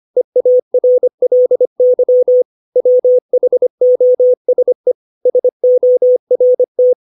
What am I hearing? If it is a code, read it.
Morse code, heard as a keyed tone, about 25 wpm, 500 Hz. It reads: EARLY WHOSE SORT
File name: 632 words / 632 Words 25WPM